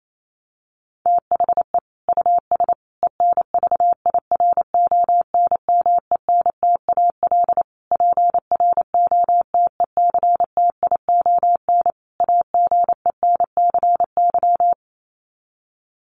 Morse code, heard as a keyed tone, about 28 wpm, 700 Hz. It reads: THE US ENVIRONMENTAL PROTECTION AGENCY
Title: Sets of 5 Words Encore2 28wpm